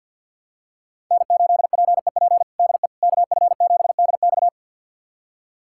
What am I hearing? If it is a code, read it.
Morse code, heard as a keyed tone, about 46 wpm, 700 Hz. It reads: N8PEP DE KF7DX
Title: USA Callsign DE Callsign - 5 Characters - 50wpm